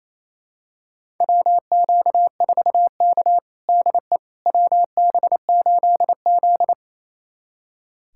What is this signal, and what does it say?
Morse code, heard as a keyed tone, about 28 wpm, 700 Hz. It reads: WQ4K DE WB8Z
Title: USA Callsign DE Callsign - 4 Characters - 28wpm